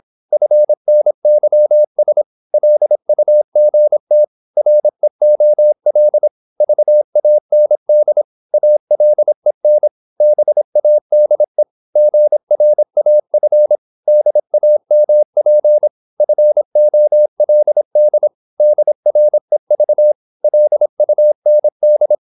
Morse code, heard as a keyed tone, about 26 wpm, 600 Hz. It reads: FNYS LUGT REOL VAND ALEN BADE GRAF DAMP FOLD DREV LUND
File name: Korte ord DK 26wpm